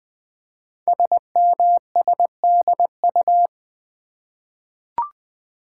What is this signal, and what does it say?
Morse code, heard as a keyed tone, about 20 wpm, 700 Hz. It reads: SMSDU
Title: Code Groups - 5 Letters - 20wpm